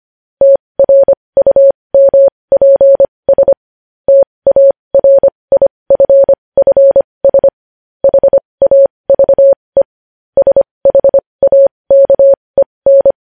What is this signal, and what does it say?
Morse code, heard as a keyed tone, about 25 wpm, spaced as for 20 wpm, 550 Hz. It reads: TRUMPS TARIFFS HAVE SHAKEN